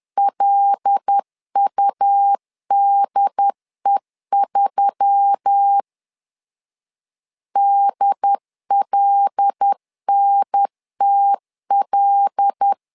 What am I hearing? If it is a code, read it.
Morse code, heard as a keyed tone, about 10 wpm, 800 Hz. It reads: LUDE3 DLNTL
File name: LEZIONI / 5 LBVC / 5 L 10wpm 2259